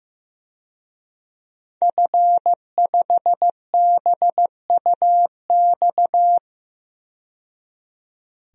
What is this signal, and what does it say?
Morse code, heard as a keyed tone, about 15 wpm, 700 Hz. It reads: F5BUX
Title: DXCC-100 Call Signs - 5 Characters - 15wpm